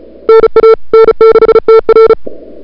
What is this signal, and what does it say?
Morse code, heard as a keyed tone, about 36 wpm, 440 Hz. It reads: NA N6TR
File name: CWSprint / February 2007 CW Sprint / 01-2007 CW Sprint Hour1